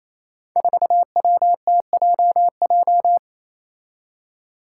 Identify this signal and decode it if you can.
Morse code, heard as a keyed tone, about 28 wpm, 700 Hz. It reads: 4WTJJ
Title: Five Challenging Letter-Number Combinations 28wpm